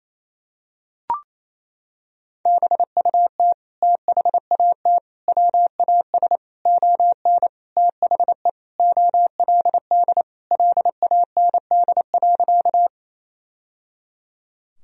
Morse code, heard as a keyed tone, about 28 wpm, 700 Hz. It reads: BUT THAT WAS ON THE OLD LAND.
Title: Sentences from Top 1000 Words - Review Part 1 - 28wpm